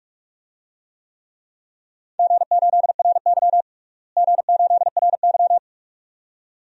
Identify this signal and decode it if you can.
Morse code, heard as a keyed tone, about 46 wpm, 700 Hz. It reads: G8RY G8RY
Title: DXCC-100 Call Signs - 4 Characters - Repeated 2x - 45wpm